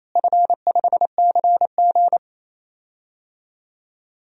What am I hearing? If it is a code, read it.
Morse code, heard as a keyed tone, about 28 wpm, 700 Hz. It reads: F5CG
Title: DXCC-100 Call Signs - 4 Characters - Repeated 3x - 28wpm